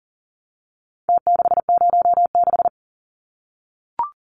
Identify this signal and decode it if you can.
Morse code, heard as a keyed tone, about 40 wpm, 700 Hz. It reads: T606
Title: Numbers - Contest Style - 4-Digit 40wpm